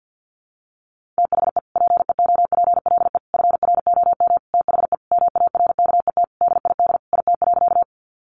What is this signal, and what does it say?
Morse code, heard as a keyed tone, about 50 wpm, 700 Hz. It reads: THE PEOPLE FROM THE MARKET DID IT.
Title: Sentences from Top 600 Words 50wpm